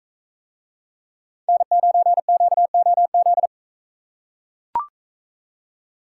Morse code, heard as a keyed tone, about 40 wpm, 700 Hz. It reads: N9QOZ